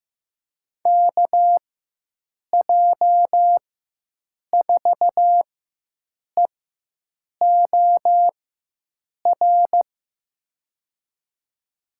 Morse code, heard as a keyed tone, about 15 wpm, spaced as for 7 wpm, 700 Hz. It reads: KJ4EOR
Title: USA Call Signs - 6 Characters - 4x Character Spacing - 15wpm